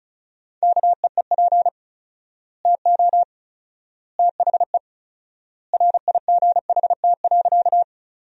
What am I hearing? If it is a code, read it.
Morse code, heard as a keyed tone, about 34 wpm, 700 Hz. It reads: KEEP TO THE RIGHT.
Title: 4x Word-Spacing - Sentences from Top 300 Words - 35wpm